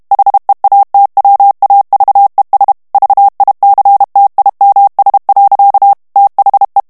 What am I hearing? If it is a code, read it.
Morse code, heard as a keyed tone, about 32 wpm, 800 Hz. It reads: HEATWAVES VICTIMS. THE
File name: ANGLAIS texte clair 2 vitesse 30 mots minute